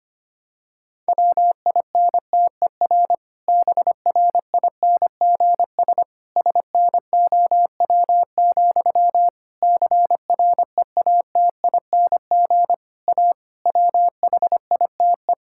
Morse code, heard as a keyed tone, about 25 wpm, 700 Hz. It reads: WINTER BRINGS SNOW, CREATING A WHITE
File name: practice / 25wpm / Difficult Sentences with Hints - 25wpm